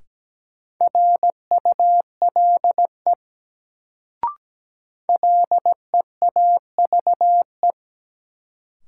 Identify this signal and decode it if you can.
Morse code, heard as a keyed tone, about 17 wpm, 700 Hz. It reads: RULE LEAVE
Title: Sentences from Top 400 Words - New Word Drill - 17wpm